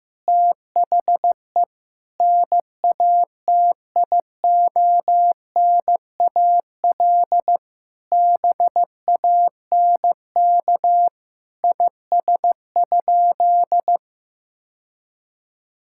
Morse code, heard as a keyed tone, about 16 wpm, 700 Hz. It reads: THE NATIONAL BANK IS?
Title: Sentences from Top 1000 Words - Review Part 6 - 15wpm